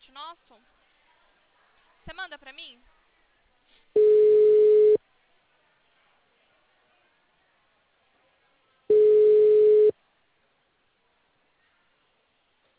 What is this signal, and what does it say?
Ringback tone (Europe, 425 Hz), with speech.